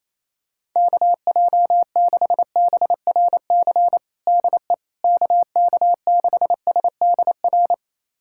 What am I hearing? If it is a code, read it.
Morse code, heard as a keyed tone, about 28 wpm, 700 Hz. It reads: KJ6BRC DE KK6SDR
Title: USA Callsign DE Callsign - 6 Characters - 28wpm